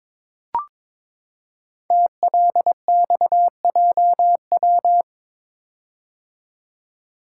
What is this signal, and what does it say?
Morse code, heard as a keyed tone, about 22 wpm, 700 Hz. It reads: TLXJW
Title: Code Groups - 5 Letters - 22wpm